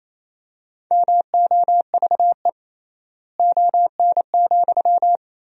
Morse code, heard as a keyed tone, about 28 wpm, 700 Hz. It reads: MOVE ON,
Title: 3x Word-Spacing - Sentences from Top 500 Words - 28wpm